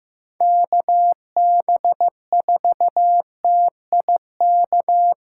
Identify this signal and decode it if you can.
Morse code, heard as a keyed tone, about 15 wpm, 700 Hz. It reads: KB4TIK
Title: USA Callsign DE Callsign - 6 Characters - 15wpm